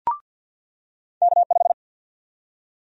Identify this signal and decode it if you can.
Morse code, heard as a keyed tone, about 53 wpm, 700 Hz. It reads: K5
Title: USA Call Sign Prefixes - 50wpm